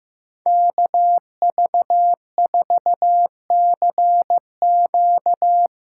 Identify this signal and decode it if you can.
Morse code, heard as a keyed tone, about 17 wpm, 700 Hz. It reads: KV4CQ
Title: Speed Racing - USA Call Signs 15wpm